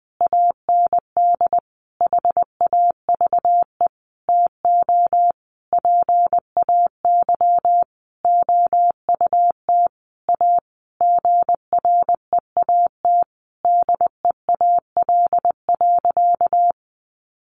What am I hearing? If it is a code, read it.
Morse code, heard as a keyed tone, about 20 wpm, 700 Hz. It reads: AND HAVE TO PAY OUT A GREAT DEAL.
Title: Sentences from Top 1500 Words 20wpm